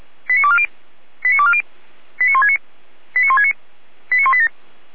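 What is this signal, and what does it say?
Five-tone selective call (ZVEI), heard as quick runs of five tones, tones of about 70 ms. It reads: 89240 89240 89169 89169 89187
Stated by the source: ZVEI five-tone selective call received by SDR, feuerwehr-orlamuende.org fire brigade alerting archive.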